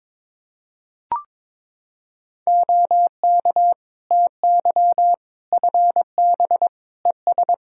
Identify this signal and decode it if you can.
Morse code, heard as a keyed tone, about 22 wpm, 700 Hz. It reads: OK TY FB ES
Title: K1USN SST QSO Practice 22wpm